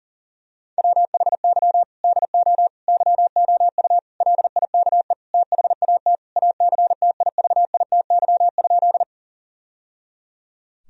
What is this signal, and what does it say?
Morse code, heard as a keyed tone, about 40 wpm, 700 Hz. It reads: WHY DO YOU LIKE THAT ACTIVITY?
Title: Sentences from Top 1000 Words - Review Part 4 - 40wpm